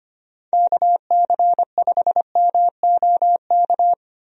Morse code, heard as a keyed tone, about 25 wpm, 700 Hz. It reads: KC5MOK
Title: USA Call Signs - 6 Characters - Repeated 3x - 25wpm